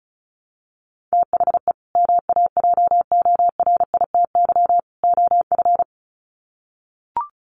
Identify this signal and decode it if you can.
Morse code, heard as a keyed tone, about 34 wpm, 700 Hz. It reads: THE MAJORITY OF
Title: Sets of 3 Words 35wpm